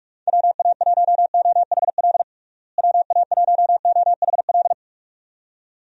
Morse code, heard as a keyed tone, about 41 wpm, 700 Hz. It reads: WA1OHL WA1OHL
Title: USA Call Signs - 6 Characters - Repeated 2x - 45wpm